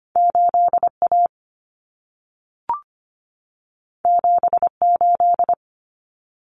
Morse code, heard as a keyed tone, about 25 wpm, 700 Hz. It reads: 8A 78
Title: Common Cut Numbers - 2-Digit 25wpm